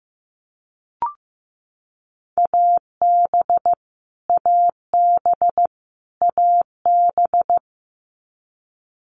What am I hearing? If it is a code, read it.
Morse code, heard as a keyed tone, about 15 wpm, 700 Hz. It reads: AB AB AB
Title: Canadian Provinces and Territories - Repeated 3x - 15wpm